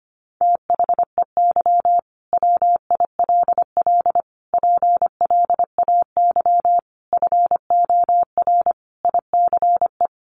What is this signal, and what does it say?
Morse code, heard as a keyed tone, about 25 wpm, 700 Hz. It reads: THEY WILL PLAY FOR ICE
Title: Sentences from Top 1500 Words 25wpm